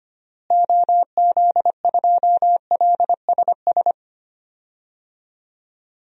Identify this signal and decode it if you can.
Morse code, heard as a keyed tone, about 25 wpm, 700 Hz. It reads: OZ2LSS
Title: DXCC-100 Call Signs - 6 Characters - Repeated 3x - 25wpm